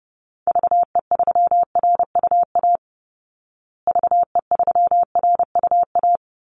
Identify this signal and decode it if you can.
Morse code, heard as a keyed tone, about 30 wpm, 700 Hz. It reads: VE3RUA VE3RUA
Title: Call Signs Repeated 2x - 30wpm